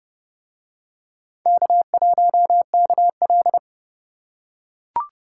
Morse code, heard as a keyed tone, about 29 wpm, 700 Hz. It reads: K1KL